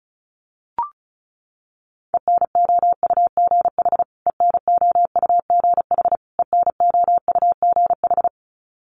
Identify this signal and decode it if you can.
Morse code, heard as a keyed tone, about 34 wpm, 700 Hz. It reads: ENOUGH ENOUGH ENOUGH
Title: Top 500 Words Repeated 35wpm